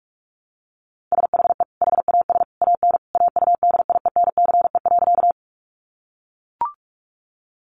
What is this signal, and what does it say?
Morse code, heard as a keyed tone, about 44 wpm, 700 Hz. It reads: SHE HAS AN AUDIENCE.
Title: Sentences from Top 800 Words 45wpm